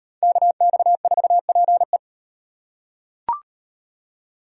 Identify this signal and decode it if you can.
Morse code, heard as a keyed tone, about 35 wpm, 700 Hz. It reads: KX4PE